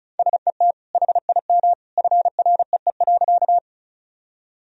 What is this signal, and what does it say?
Morse code, heard as a keyed tone, about 34 wpm, 700 Hz. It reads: SET HIM FREE.
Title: Sentences from Top 1000 Words - Review Part 1 - 35wpm